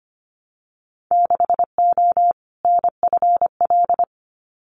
Morse code, heard as a keyed tone, about 25 wpm, 700 Hz. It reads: 6O NFL
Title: Winter Field Day Exchanges 25wpm